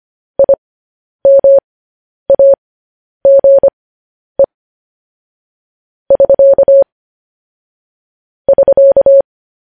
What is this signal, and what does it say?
Morse code, heard as a keyed tone, about 25 wpm, spaced as for 10 wpm, 550 Hz. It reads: IMAGE <SK> <SK>
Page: News Headlines in Morse Code at 10 WPM